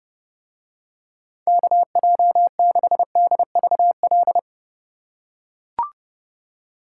Morse code, heard as a keyed tone, about 27 wpm, 700 Hz. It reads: KJ6DVL